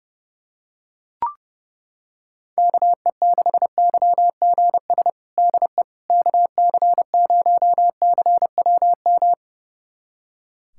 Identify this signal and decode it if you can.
Morse code, heard as a keyed tone, about 30 wpm, 700 Hz. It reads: KE6YGS DE KC0CWM
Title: USA Callsign DE Callsign 30wpm